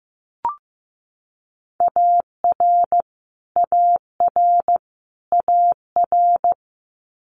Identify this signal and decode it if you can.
Morse code, heard as a keyed tone, about 15 wpm, 700 Hz. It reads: AR AR AR
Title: Multi-letter Phonemes Repeated 3x 15wpm